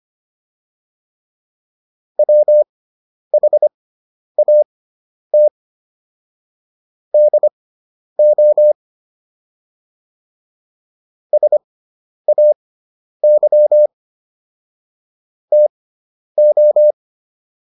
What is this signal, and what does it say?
Morse code, heard as a keyed tone, about 25 wpm, spaced as for 10 wpm, 600 Hz. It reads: WHAT DO SAY TO
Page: intermediate-practice-files